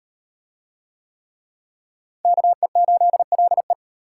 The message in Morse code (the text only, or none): KE8LE